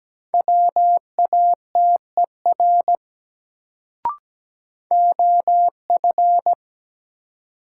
Morse code, heard as a keyed tone, about 17 wpm, 700 Hz. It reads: WATER OF
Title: Sentences from Top 100 Words - New Word Drill - 17wpm